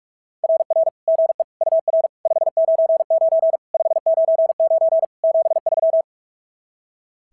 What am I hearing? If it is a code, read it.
Morse code, heard as a keyed tone, about 44 wpm, 690 Hz. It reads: RR GE UR 599 599 73
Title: SOTA QSO Practice 45wpm